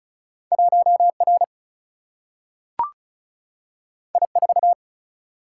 Morse code, heard as a keyed tone, about 36 wpm, 700 Hz. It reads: 1R I4
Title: Single-Letter Single-Number 35wpm